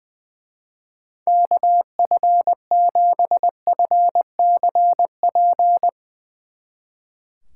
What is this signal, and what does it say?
Morse code, heard as a keyed tone, about 23 wpm, 700 Hz. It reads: KF7FCP
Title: Speed Racing - USA Call Signs 20wpm